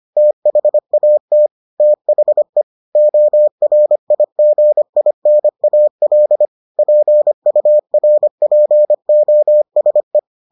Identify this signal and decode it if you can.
Morse code, heard as a keyed tone, about 25 wpm, 600 Hz. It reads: THAT THE ORIGINAL PURPOSE